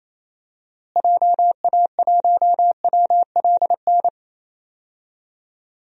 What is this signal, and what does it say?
Morse code, heard as a keyed tone, about 27 wpm, 700 Hz. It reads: JA1WLN
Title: DXCC-100 Call Signs - 6 Characters - Repeated 2x - 28wpm